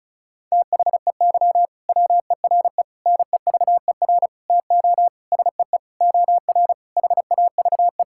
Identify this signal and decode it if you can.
Morse code, heard as a keyed tone, about 36 wpm, 700 Hz. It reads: THEY WERE NEVER TO SEE OR HAVE
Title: Sentences from Top 1000 Words - Review Part 6 - 35wpm